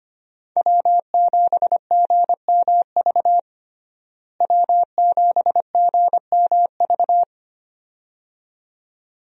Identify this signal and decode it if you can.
Morse code, heard as a keyed tone, about 25 wpm, 700 Hz. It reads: W7GMV W7GMV
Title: USA Call Signs - 5 Characters - Repeated 2x - 25wpm